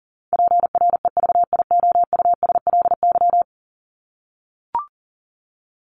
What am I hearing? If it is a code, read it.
Morse code, heard as a keyed tone, about 40 wpm, 700 Hz. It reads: PREVIOUSLY
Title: LY Suffix - Top 100 Words with Base Words 40wpm